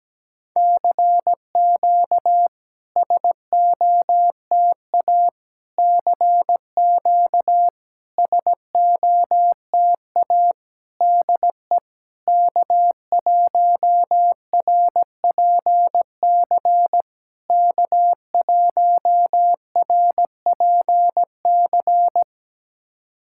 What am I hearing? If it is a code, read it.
Morse code, heard as a keyed tone, about 17 wpm, 700 Hz. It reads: CQ SOTA CQ SOTA DE K1RPC K1RPC
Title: SOTA QSO Practice 17wpm